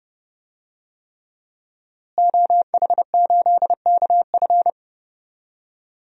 Morse code, heard as a keyed tone, about 30 wpm, 700 Hz. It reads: OH8KF
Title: DXCC-100 Call Signs - 5 Characters - Repeated 3x - 30wpm